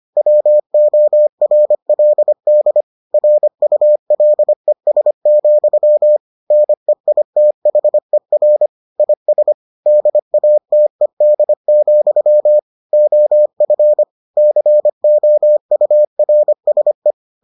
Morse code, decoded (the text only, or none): WORLD RULES, NEITHER IS DATED, OF COURSE